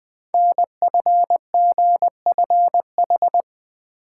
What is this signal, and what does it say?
Morse code, heard as a keyed tone, about 20 wpm, 700 Hz. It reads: NFGFH
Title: Code Groups - 5 Letters - 20wpm